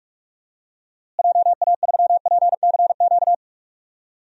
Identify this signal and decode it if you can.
Morse code, heard as a keyed tone, about 44 wpm, 700 Hz. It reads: JA3PCQ
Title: DXCC-100 Call Signs - 6 Characters - Repeated 3x - 45wpm